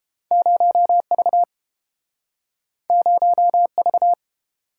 Morse code, heard as a keyed tone, about 31 wpm, 700 Hz. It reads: 0V 0V